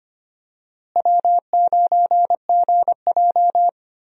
Morse code, heard as a keyed tone, about 25 wpm, 700 Hz. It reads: W9GJ
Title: USA Call Signs - 4 Characters - Repeated 3x - 25wpm